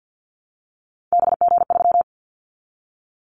Morse code, heard as a keyed tone, about 50 wpm, 690 Hz. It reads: BG3
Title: Callsign Prefixes for Top 100 DXCC - 50wpm